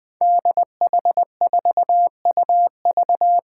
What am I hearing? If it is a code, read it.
Morse code, heard as a keyed tone, about 20 wpm, 700 Hz. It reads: DH4UV
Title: Five Challenging Letter-Number Combinations 20wpm